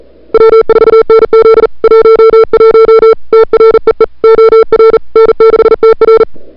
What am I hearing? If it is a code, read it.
Morse code, heard as a keyed tone, about 37 wpm, 440 Hz. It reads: W4NZ 11 TREE OR N6TR
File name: CWSprint / February 2007 CW Sprint / 01-2007 CW Sprint Hour1